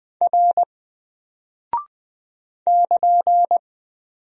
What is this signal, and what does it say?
Morse code, heard as a keyed tone, about 20 wpm, 700 Hz. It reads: R <KN>
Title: Single Character (Letter:Number:Punctuation:Prosign) 20wpm